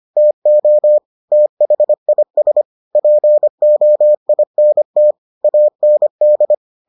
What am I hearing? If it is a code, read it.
Morse code, heard as a keyed tone, about 25 wpm, 600 Hz. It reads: TO THIS POINT AND